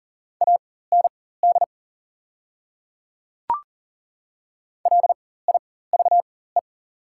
Morse code, heard as a keyed tone, about 39 wpm, spaced as for 18 wpm, 700 Hz. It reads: AND LIVE
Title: 4x Character Spacing - Top 300 Words - 40wpm